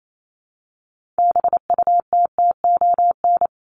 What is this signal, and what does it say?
Morse code, heard as a keyed tone, about 28 wpm, 700 Hz. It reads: BUTTON
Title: Top 1001-1250 Words 28wpm